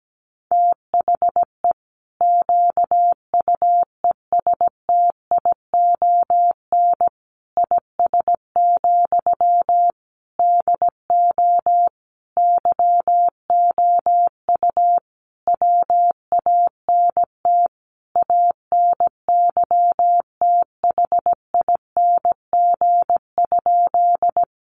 Morse code, heard as a keyed tone, about 17 wpm, 700 Hz. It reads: THE QUESTION IS, DO YOU WANT ANYTHING?
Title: Sentences from Top 1000 Words 17wpm